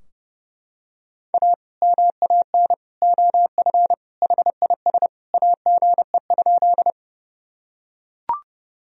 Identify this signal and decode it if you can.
Morse code, heard as a keyed tone, about 30 wpm, 700 Hz. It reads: A MAN OF HIS AGE?
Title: Sentences from Top 1000 Words - Review Part 3 - 30wpm